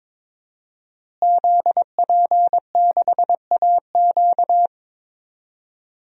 Morse code, heard as a keyed tone, about 22 wpm, 700 Hz. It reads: ZP6AQ
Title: DXCC-100 Call Signs - 5 Characters - Repeated 2x - 22wpm